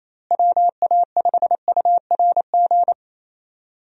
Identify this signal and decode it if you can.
Morse code, heard as a keyed tone, about 28 wpm, 700 Hz. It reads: WA5URG
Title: USA Call Signs - 6 Characters - Repeated 2x - 28wpm